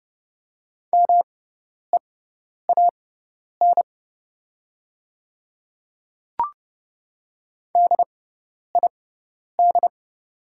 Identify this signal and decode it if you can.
Morse code, heard as a keyed tone, about 30 wpm, spaced as for 10 wpm, 690 Hz. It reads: MEAN DID
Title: 6x Character Spacing - Top 200 Words - 30wpm